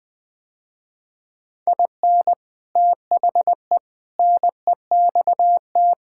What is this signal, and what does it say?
Morse code, heard as a keyed tone, about 20 wpm, 700 Hz. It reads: IN THE NEXT
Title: Sentences with 3-Word Phrases - Spoken Beginning - 20wpm